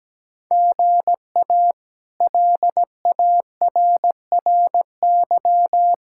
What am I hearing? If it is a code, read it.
Morse code, heard as a keyed tone, about 17 wpm, 700 Hz. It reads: GA LARRY
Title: K1USN SST QSO Practice - 17wpm